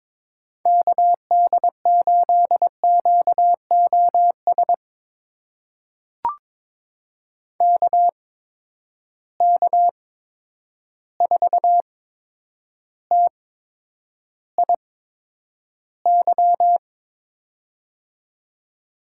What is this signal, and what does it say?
Morse code, heard as a keyed tone, about 22 wpm, spaced as for 9 wpm, 700 Hz. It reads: KD8QOS KK4TIY
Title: USA Call Signs - 8x Character Spacing - 22wpm